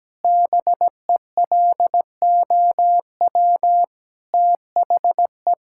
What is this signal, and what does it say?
Morse code, heard as a keyed tone, about 17 wpm, 700 Hz. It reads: BELOW THE
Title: Sentences from Top 1000 Words - Review Part 5 - 17wpm